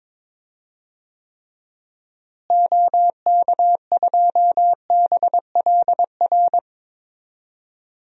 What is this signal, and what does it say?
Morse code, heard as a keyed tone, about 21 wpm, 700 Hz. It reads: OK2BLR
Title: DXCC-100 Call Signs - 6 Characters - Repeated 3x - 22wpm